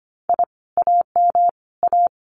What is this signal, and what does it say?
Morse code, heard as a keyed tone, about 25 wpm, 700 Hz. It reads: I AM A
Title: Common QSO Phrases Encore 25wpm